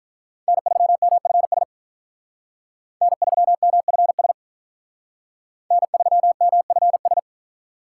Morse code, heard as a keyed tone, about 45 wpm, 700 Hz. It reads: N3MFS N3MFS N3MFS